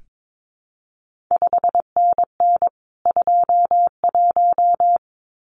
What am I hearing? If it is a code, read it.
Morse code, heard as a keyed tone, about 22 wpm, 700 Hz. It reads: 5NN 21
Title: CQ WW DX Contest Exchange Encore - 22wpm